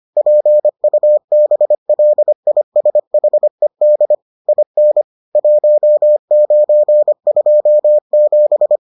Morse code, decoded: PUBLISHED IN 1927

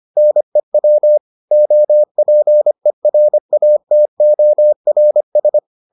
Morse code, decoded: NEW OPERATORS

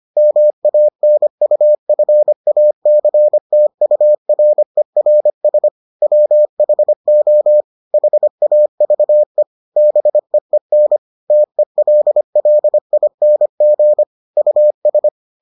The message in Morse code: MANUFACTURERS WHO HAVE BEEN TELLING US

